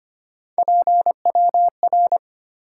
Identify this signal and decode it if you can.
Morse code, heard as a keyed tone, about 25 wpm, 700 Hz. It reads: PWR
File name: Top 90 QSO Elements - 4x Character Spacing - 25wpm